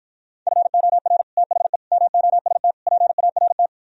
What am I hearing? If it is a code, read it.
Morse code, heard as a keyed tone, about 57 wpm, 700 Hz. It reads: FOR THE MOST PART